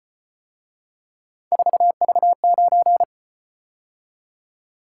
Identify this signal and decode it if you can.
Morse code, heard as a keyed tone, about 32 wpm, 700 Hz. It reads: 4V9